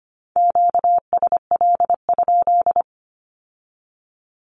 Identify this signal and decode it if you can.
Morse code, heard as a keyed tone, about 25 wpm, 700 Hz. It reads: QSL?
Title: Q-Codes with Spoken Meaning - 25wpm